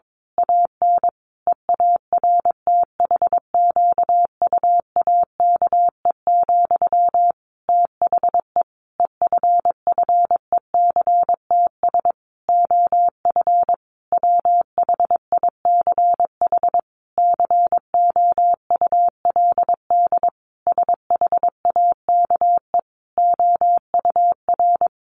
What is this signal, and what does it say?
Morse code, heard as a keyed tone, about 22 wpm, 700 Hz. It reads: AN EARTHQUAKE, THE EFFECTS OF WHICH COULD SHAKE OUR
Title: A Sound Of Thunder 22wpm